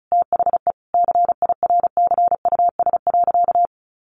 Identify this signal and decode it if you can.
Morse code, heard as a keyed tone, about 34 wpm, 700 Hz. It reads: THE CIRCUS.
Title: Easy Phrases 35wpm